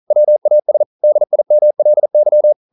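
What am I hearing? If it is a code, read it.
Morse code, heard as a keyed tone, about 40 wpm, 600 Hz. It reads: WAS DIMLY
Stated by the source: Morse code practice at 41 wpm, from CW med Gnister.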